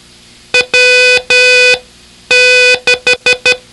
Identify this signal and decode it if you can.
Morse code, heard as a keyed tone, about 12 wpm, spaced as for 9 wpm, 2500 Hz, with noise. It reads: W6
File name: ham radio / CodeQuick / Code Quick 01